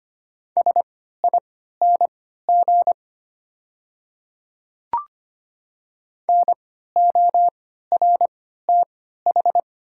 Morse code, heard as a keyed tone, about 25 wpm, spaced as for 14 wpm, 690 Hz. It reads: SING NORTH
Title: 3x Character Spacing - Top 500 Words - 25wpm